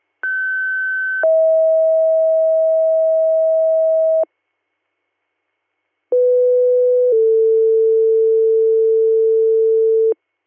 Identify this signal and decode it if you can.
Two-tone page, heard as a one-second tone, then a three-second tone, about 1550 then 640 Hz, about 500 then 450 Hz. It is heard from 0.2 to 10.1 s.